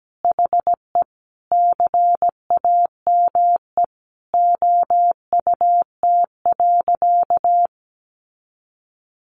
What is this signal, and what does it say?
Morse code, heard as a keyed tone, about 17 wpm, 700 Hz. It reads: HE CAME OUT.
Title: Sentences from Top 1300 Words 17wpm